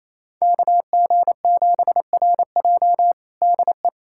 Morse code, heard as a keyed tone, about 28 wpm, 700 Hz. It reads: KG7RJ DE
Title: USA Callsign DE Callsign - 5 Characters - 28wpm